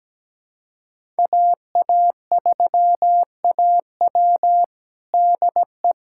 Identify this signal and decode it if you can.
Morse code, heard as a keyed tone, about 17 wpm, 700 Hz. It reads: AA3AW DE
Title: USA Callsign DE Callsign - 5 Characters - 17wpm